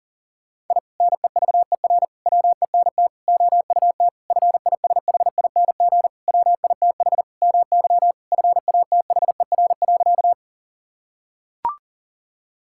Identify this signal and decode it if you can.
Morse code, heard as a keyed tone, about 40 wpm, 700 Hz. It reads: I NEVER WENT OUT FISHING WITH MY FATHER.
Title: Sentences from Top 1000 Words - Review Part 5 - 40wpm